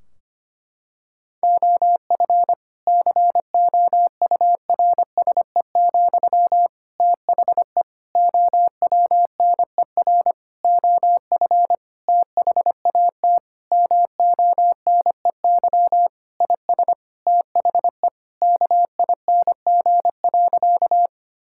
Morse code, heard as a keyed tone, about 25 wpm, 700 Hz. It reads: OF COURSE, THE OWNER OF THAT MONEY IS THE KING.
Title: Sentences from Top 1600 Words 25wpm